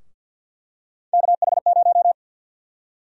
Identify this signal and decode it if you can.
Morse code, heard as a keyed tone, about 53 wpm, 700 Hz. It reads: KH0
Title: USA Call Sign Prefixes - 50wpm